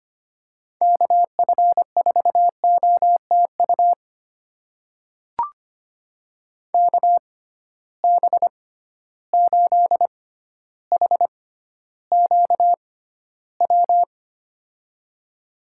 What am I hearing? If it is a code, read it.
Morse code, heard as a keyed tone, about 25 wpm, spaced as for 13 wpm, 700 Hz. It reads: KF4OTU KB8HQW
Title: USA Call Signs - 6 Characters - 6x Character Spacing - 25wpm